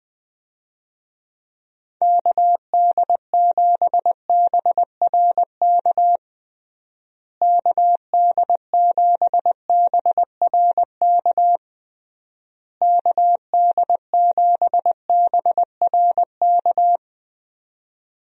Morse code, heard as a keyed tone, about 21 wpm, 700 Hz. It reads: KD7BRK KD7BRK KD7BRK
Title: USA Call Signs - 6 Characters - Repeated 3x - 20wpm